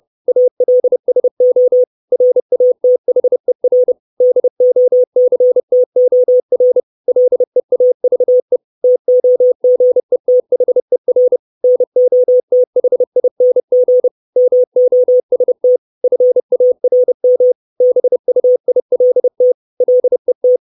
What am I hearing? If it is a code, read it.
Morse code, heard as a keyed tone, about 30 wpm, 500 Hz. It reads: ALSO RATHER DOCTOR LEAVE TOGETHER NOTHING MOST FARM BUILT LET
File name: 632 Words 30WPM